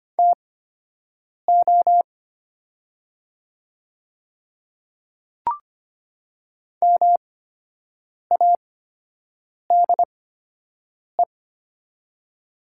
Morse code, heard as a keyed tone, about 25 wpm, spaced as for 7 wpm, 700 Hz. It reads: TO MADE
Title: 8x Character Spacing - Top 200 Words - 25wpm